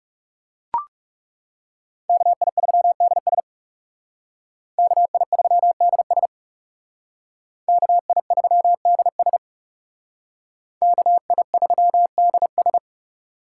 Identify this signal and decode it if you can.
Morse code, heard as a keyed tone, about 38 wpm, 700 Hz. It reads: KI3DS KI3DS KI3DS KI3DS